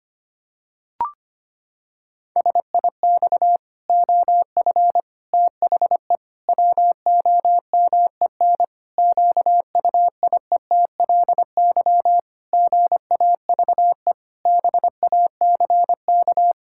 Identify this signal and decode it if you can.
Morse code, heard as a keyed tone, about 25 wpm, 700 Hz. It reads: SIX OF THE WOMEN QUIETLY GAVE BACK
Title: Pangram Challenge v2 - 25wpm